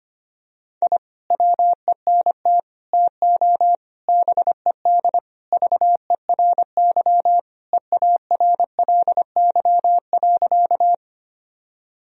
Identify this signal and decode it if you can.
Morse code, heard as a keyed tone, about 25 wpm, 700 Hz. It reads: I WENT TO BED VERY EARLY.
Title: Sentences from Top 1000 Words - Review Part 2 - 25wpm